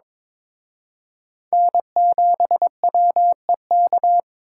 Morse code, heard as a keyed tone, about 26 wpm, 700 Hz. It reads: N7WEK
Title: Speed Racing - USA 1x3 Call Signs 22wpm